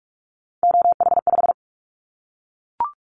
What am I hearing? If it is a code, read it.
Morse code, heard as a keyed tone, about 44 wpm, 700 Hz. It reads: OH5
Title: Callsign Prefixes for Top 25 DXCC - Repeated 3x - 45wpm